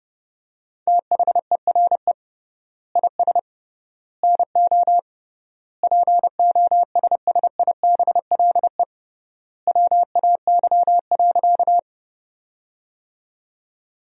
Morse code, heard as a keyed tone, about 30 wpm, 700 Hz. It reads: THERE IS NO POSSIBLE WAY.
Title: 3x Word-Spacing - Sentences from Top 500 Words - 30wpm